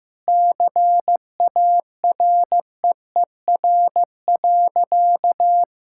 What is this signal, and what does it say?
Morse code, heard as a keyed tone, about 15 wpm, 700 Hz. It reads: CAREER.
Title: Sentences from Top 1000 Words - Review Part 6 - 15wpm